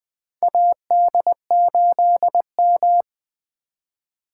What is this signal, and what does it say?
Morse code, heard as a keyed tone, about 20 wpm, 700 Hz. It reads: AD8M
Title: USA Call Signs - 4x Character Spacing - 20wpm